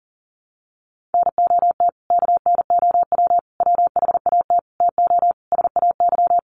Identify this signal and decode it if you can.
Morse code, heard as a keyed tone, about 40 wpm, 700 Hz. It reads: NOT KNOW WHAT TO SAY
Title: Sets of 5 Words 40wpm